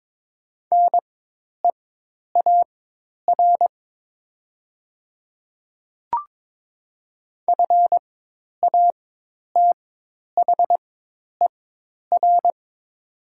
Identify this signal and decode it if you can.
Morse code, heard as a keyed tone, about 22 wpm, spaced as for 10 wpm, 700 Hz. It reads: NEAR FATHER
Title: 4x Character Spacing - Top 500 Words - 22wpm